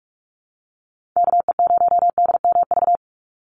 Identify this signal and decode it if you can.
Morse code, heard as a keyed tone, about 46 wpm, 700 Hz. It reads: KE0DMV
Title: USA Call Signs 45wpm